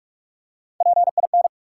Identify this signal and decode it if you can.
Morse code, heard as a keyed tone, about 44 wpm, 700 Hz. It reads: PIN
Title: Three Letter Words - 45wpm